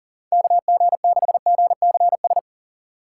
Morse code, heard as a keyed tone, about 39 wpm, 700 Hz. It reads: KG6GCS
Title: USA Call Signs - 6 Characters - 2x Character Spacing - 40wpm